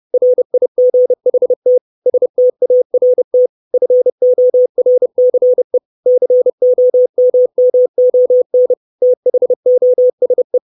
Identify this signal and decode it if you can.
Morse code, heard as a keyed tone, about 30 wpm, 500 Hz. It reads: RIGHT START FORCE COMMON THOSE
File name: 632 words / 632 Words 30WPM